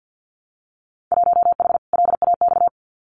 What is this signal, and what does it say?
Morse code, heard as a keyed tone, about 50 wpm, 700 Hz. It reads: 1H LAX
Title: Winter Field Day Exchanges 50wpm